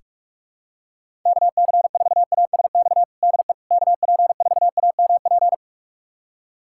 Morse code, heard as a keyed tone, about 41 wpm, 700 Hz. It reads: KC4ASX DE KP4AMP
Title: USA Callsign DE Callsign 45wpm